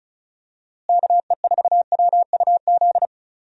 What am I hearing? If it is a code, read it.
Morse code, heard as a keyed tone, about 36 wpm, 700 Hz. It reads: KE4WUZ
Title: USA Call Signs - 6 Characters - Repeated 3x - 35wpm